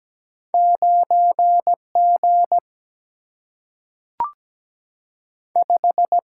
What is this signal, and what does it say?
Morse code, heard as a keyed tone, about 17 wpm, 700 Hz. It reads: 9 G 5
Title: Single-Letter Single-Number 17wpm